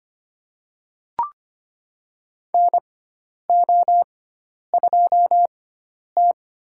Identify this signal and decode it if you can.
Morse code, heard as a keyed tone, about 25 wpm, spaced as for 10 wpm, 700 Hz. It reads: NO2T
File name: Lesson 16g - Letter c - Call Signs 25wpm